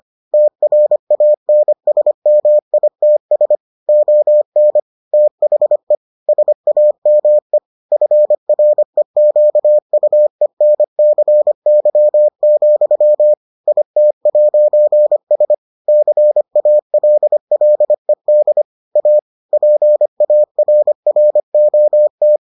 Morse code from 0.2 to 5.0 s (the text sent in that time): TRANSMITS ON